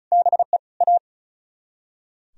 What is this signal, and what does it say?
Morse code, heard as a keyed tone, about 45 wpm, 700 Hz. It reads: BE A